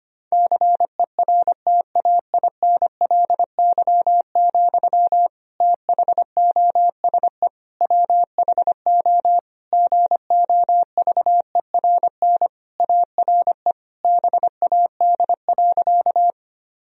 Morse code, heard as a keyed tone, about 25 wpm, 700 Hz. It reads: CERTAINLY, THOSE WHO GOVERN ARE BAD.
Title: Sentences from Top 1000 Words - Review Part 2 - 25wpm